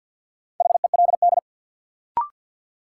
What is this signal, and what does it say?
Morse code, heard as a keyed tone, about 52 wpm, 700 Hz. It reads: HELD
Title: Sentences from Top 1600 Words - New Word Drill Alt - 50wpm